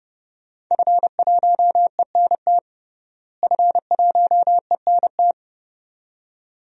Morse code, heard as a keyed tone, about 30 wpm, 700 Hz. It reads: F1ENT F1ENT
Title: DXCC-100 Call Signs - 5 Characters - Repeated 2x - 30wpm